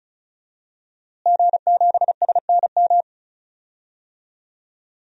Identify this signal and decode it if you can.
Morse code, heard as a keyed tone, about 36 wpm, 700 Hz. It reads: G7SNM
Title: DXCC-100 Call Signs - 5 Characters - Repeated 2x - 35wpm